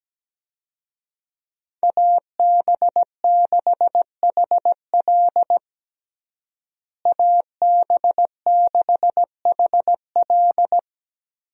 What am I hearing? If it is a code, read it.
Morse code, heard as a keyed tone, about 17 wpm, 700 Hz. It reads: AB6HL AB6HL
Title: USA Call Signs - 5 Characters - Repeated 3x - 17wpm